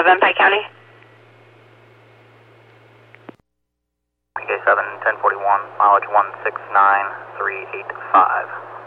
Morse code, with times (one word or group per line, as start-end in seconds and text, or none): none